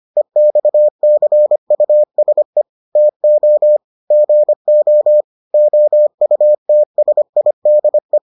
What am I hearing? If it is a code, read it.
Morse code, heard as a keyed tone, about 25 wpm, 600 Hz. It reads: EXCUSE TO GO OUTSIDE